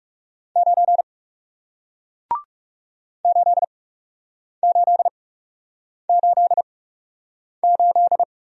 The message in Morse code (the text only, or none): 9 8888